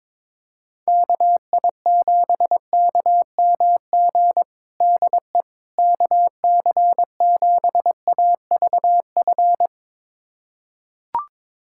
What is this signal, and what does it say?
Morse code, heard as a keyed tone, about 22 wpm, 700 Hz. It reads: KI7KMG DE KC7AVF
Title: USA Callsign DE Callsign - 6 Characters - 22wpm